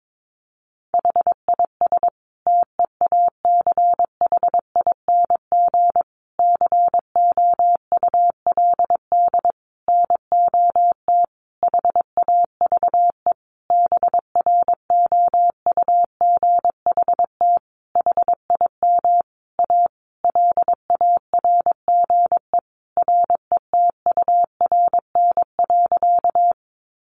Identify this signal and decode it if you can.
Morse code, heard as a keyed tone, about 22 wpm, 700 Hz. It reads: HIS TEACHING COULD NOT HAVE BROUGHT HIM A LARGE RETURN.
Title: Sentences from Top 1100 Words 22wpm